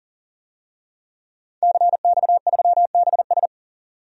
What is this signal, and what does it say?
Morse code, heard as a keyed tone, about 37 wpm, 700 Hz. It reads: CX3BS
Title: DXCC-100 Call Signs - 5 Characters - 40wpm